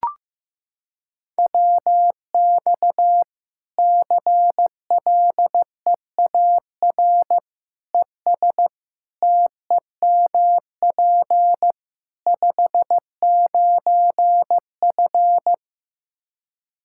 Morse code, heard as a keyed tone, about 16 wpm, 700 Hz. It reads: WX CLEAR ES TEMP 59F
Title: Practice QSO - Exchange 2 - Info - 15wpm